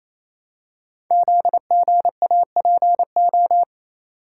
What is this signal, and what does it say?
Morse code, heard as a keyed tone, about 27 wpm, 700 Hz. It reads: ZGAPO
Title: Code Groups - 5 Letters - 28wpm